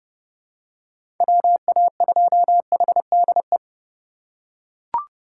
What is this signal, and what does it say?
Morse code, heard as a keyed tone, about 30 wpm, 700 Hz. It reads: WA2HDE
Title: USA Call Signs - 6 Characters - 2x Character Spacing - 30wpm